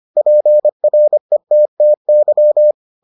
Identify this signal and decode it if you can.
Morse code, heard as a keyed tone, about 25 wpm, 600 Hz. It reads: PRETTY